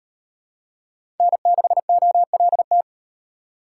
Morse code, heard as a keyed tone, about 38 wpm, 700 Hz. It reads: N6OLT